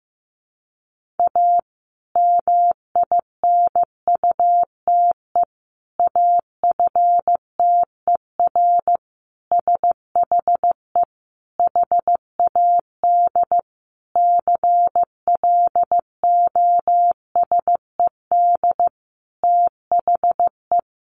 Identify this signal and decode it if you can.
Morse code, heard as a keyed tone, about 15 wpm, 700 Hz. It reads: A MINUTE AFTER SHE HAD CLOSED THE
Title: Sentences from Top 1400 Words 15wpm